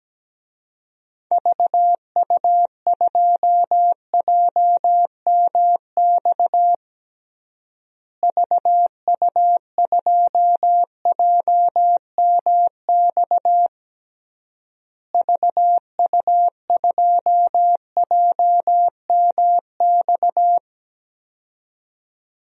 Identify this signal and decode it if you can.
Morse code, heard as a keyed tone, about 17 wpm, 700 Hz. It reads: VU2JMX VU2JMX VU2JMX
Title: DXCC-100 Call Signs - 6 Characters - Repeated 3x - 17wpm